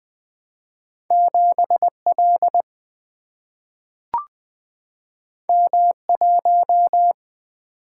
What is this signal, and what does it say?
Morse code, heard as a keyed tone, about 20 wpm, 700 Hz. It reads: 7 L M 1
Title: Single-Letter Single-Number 20wpm